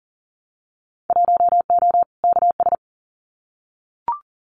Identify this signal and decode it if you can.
Morse code, heard as a keyed tone, about 38 wpm, 700 Hz. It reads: 1O KS
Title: Winter Field Day Exchanges 40wpm